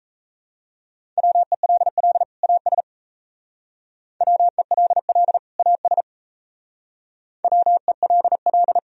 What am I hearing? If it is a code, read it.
Morse code, heard as a keyed tone, about 36 wpm, 690 Hz. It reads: WELL AS WELL AS WELL